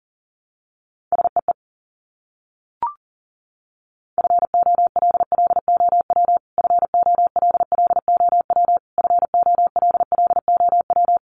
Morse code, heard as a keyed tone, about 40 wpm, 700 Hz. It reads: SEE FOLLOW FOLLOW FOLLOW
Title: Top 500 Words Repeated 40wpm